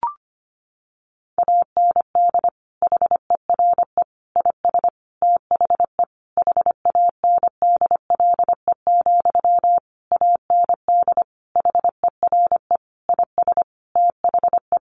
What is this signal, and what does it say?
Morse code, heard as a keyed tone, about 25 wpm, 700 Hz. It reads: AND HERE IS THE HANDLE, AND HERE IS THE
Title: Sentences from Top 1100 Words 25wpm